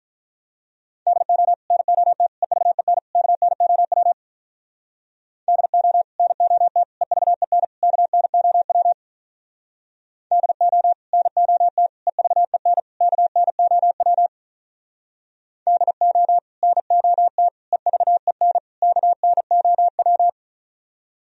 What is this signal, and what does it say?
Morse code, heard as a keyed tone, about 46 wpm, 700 Hz. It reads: DO NOT EVEN KNOW DO NOT EVEN KNOW DO NOT EVEN KNOW DO NOT EVEN KNOW